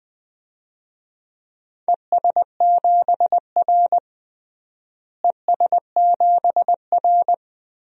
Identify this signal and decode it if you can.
Morse code, heard as a keyed tone, about 21 wpm, 700 Hz. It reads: ES7R ES7R
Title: DXCC-100 Call Signs - 4 Characters - Repeated 2x - 20wpm